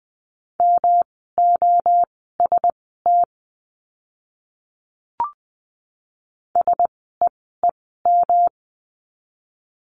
Morse code, heard as a keyed tone, about 20 wpm, spaced as for 14 wpm, 700 Hz. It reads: MOST SEEM
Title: 230 Lesson 37f - Letter x - Words 20wpm